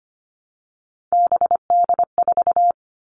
Morse code, heard as a keyed tone, about 25 wpm, 700 Hz. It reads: BD4
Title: Callsign Prefixes for Top 100 DXCC - 25wpm